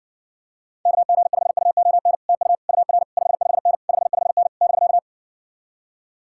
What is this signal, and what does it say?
Morse code, heard as a keyed tone, about 60 wpm, 700 Hz. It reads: KG5UON TU UR 55N 55N <BK>